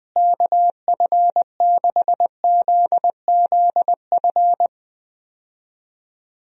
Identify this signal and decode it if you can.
Morse code, heard as a keyed tone, about 20 wpm, 700 Hz. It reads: KF6ZZF
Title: USA Call Signs - 6 Characters - Repeated 3x - 20wpm